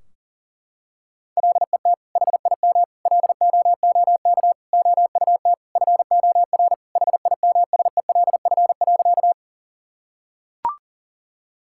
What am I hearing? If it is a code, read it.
Morse code, heard as a keyed tone, about 37 wpm, 700 Hz. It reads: LET HIM LOOK OUT FOR HIMSELF.
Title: Sentences from Top 1600 Words 40wpm